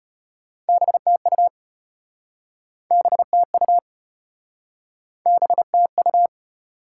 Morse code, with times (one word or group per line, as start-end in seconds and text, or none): 0.7-1.5 BTU
2.9-3.8 BTU
5.3-6.3 BTU